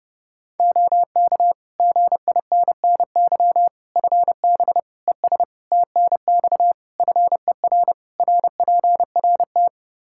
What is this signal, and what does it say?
Morse code, heard as a keyed tone, about 30 wpm, 700 Hz. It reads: OK GINNY FB ES TNX FER RPRT
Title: Practice QSO - Exchange 2 - Info - 30wpm